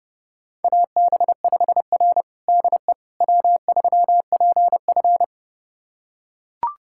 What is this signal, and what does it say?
Morse code, heard as a keyed tone, about 30 wpm, 700 Hz. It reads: AB5R DE W3PF
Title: USA Callsign DE Callsign - 4 Characters - 30wpm